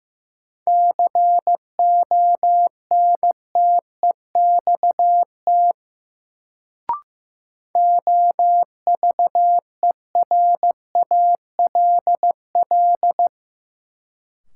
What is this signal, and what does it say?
Morse code, heard as a keyed tone, about 15 wpm, 700 Hz. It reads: CONTEXT OVERALL
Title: Sentences from Top 1000 Words - Review Part 7 - New Word Drill - 15wpm